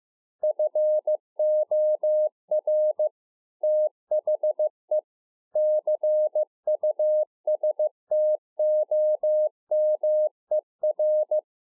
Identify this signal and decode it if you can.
Morse code, heard as a keyed tone, about 15 wpm, 610 Hz. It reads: FOR THE CUSTOMER